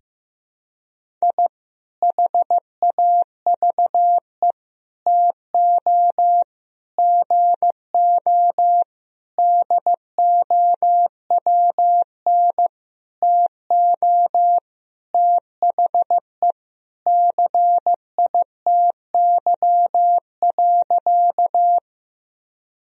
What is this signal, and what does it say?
Morse code, heard as a keyed tone, about 15 wpm, 700 Hz. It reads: I HAVE TO GO DOWN TO THE CITY.
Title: Sentences from Top 1000 Words - Review Part 1 - 15wpm